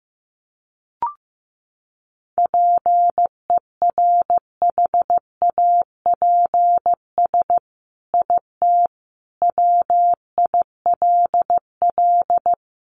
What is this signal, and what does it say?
Morse code, heard as a keyed tone, about 15 wpm, 700 Hz. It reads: PERHAPS IT WILL
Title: Sentences from Top 1300 Words 15wpm